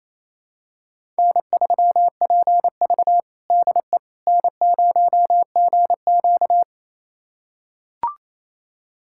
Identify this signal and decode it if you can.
Morse code, heard as a keyed tone, about 28 wpm, 700 Hz. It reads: N3PV DE N0GQ
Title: USA Callsign DE Callsign - 4 Characters - 28wpm